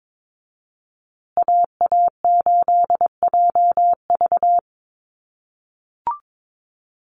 Morse code, heard as a keyed tone, about 22 wpm, 700 Hz. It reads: AA8JV
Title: USA Call Signs Repeated 2x 22wpm